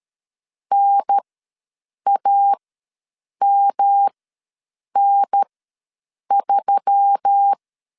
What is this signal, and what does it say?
Morse code, heard as a keyed tone, about 13 wpm, spaced as for 7 wpm, 800 Hz. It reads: NAMN3
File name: LEZIONI / 5 LBVC / 5 L ws 12wpm 2958